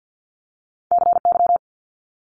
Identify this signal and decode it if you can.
Morse code, heard as a keyed tone, about 50 wpm, 700 Hz. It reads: CY
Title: Multi-letter Phonemes Quick Fire 50wpm